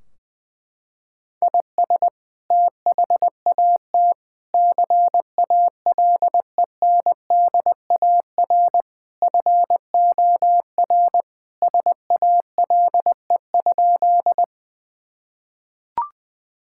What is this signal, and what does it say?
Morse code, heard as a keyed tone, about 21 wpm, 700 Hz. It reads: IS THAT CALENDAR FOR SALE?
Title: Sentences from Top 1600 Words 20wpm